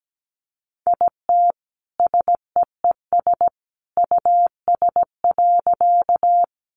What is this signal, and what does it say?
Morse code, heard as a keyed tone, about 17 wpm, 700 Hz. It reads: IT SEES US.
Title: A Sound Of Thunder 17wpm